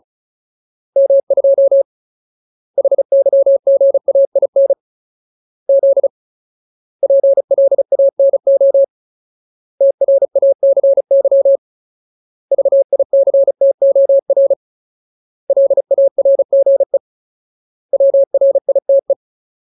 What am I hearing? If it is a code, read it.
Morse code, heard as a keyed tone, about 34 wpm, 550 Hz. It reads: M2 HYGAIN Z PLANO TRACY VICTOR LARGE WRITE
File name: Ultimate Word List / 35WPM Ultimate